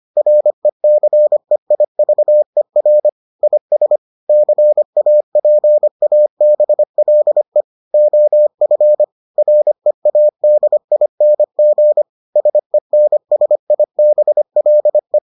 Morse code, heard as a keyed tone, about 25 wpm, 600 Hz. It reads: RECEIVER IS CAPABLE OF READING SENSIBLE